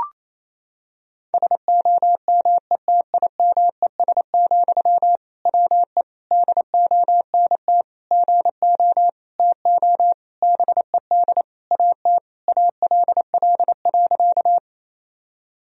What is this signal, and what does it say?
Morse code, heard as a keyed tone, about 27 wpm, 700 Hz. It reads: SOMETIMES, WE DONT GO TO BED AT ALL.
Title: Sentences from Top 1000 Words - Review Part 3 - 28wpm